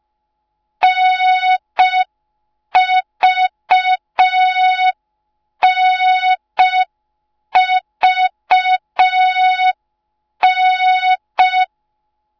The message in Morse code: NVNVN